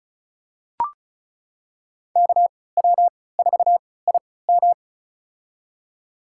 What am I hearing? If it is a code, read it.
Morse code, heard as a keyed tone, about 36 wpm, spaced as for 20 wpm, 700 Hz. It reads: KW4IM
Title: USA Call Signs - 3x Character Spacing - 35wpm